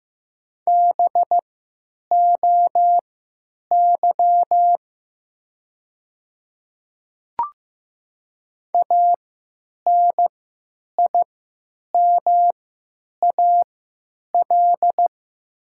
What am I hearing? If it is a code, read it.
Morse code, heard as a keyed tone, about 15 wpm, spaced as for 9 wpm, 700 Hz. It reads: BOY ANIMAL
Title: 3x Character Spacing - Top 300 Words - 15wpm